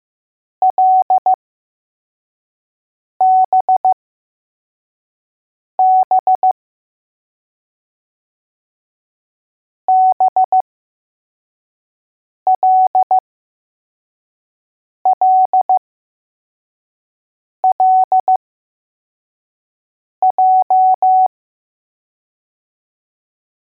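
Morse code, heard as a keyed tone, about 15 wpm, spaced as for 4 wpm, 750 Hz. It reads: LBB BLLLJ
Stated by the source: Morse code letters test drill for L, J and B.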